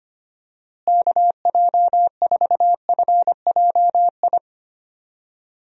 Morse code, heard as a keyed tone, about 25 wpm, 700 Hz. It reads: KJ4FJI
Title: USA Call Signs - 6 Characters - Repeated 2x - 25wpm